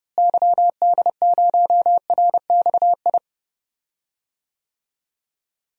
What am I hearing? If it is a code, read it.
Morse code, heard as a keyed tone, about 31 wpm, 700 Hz. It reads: YD0RXI
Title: DXCC-100 Call Signs - 6 Characters - Repeated 2x - 30wpm